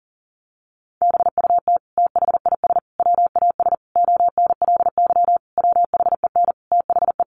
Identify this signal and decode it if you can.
Morse code, heard as a keyed tone, about 40 wpm, 700 Hz. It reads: BUT THIS WAS ONLY WHEN THE
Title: Sentences from Top 1300 Words 40wpm